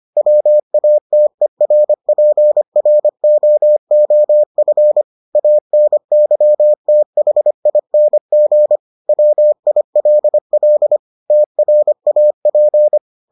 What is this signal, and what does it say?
Morse code, heard as a keyed tone, about 25 wpm, 600 Hz. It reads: WATERPROOF ANYTHING WILL TRAP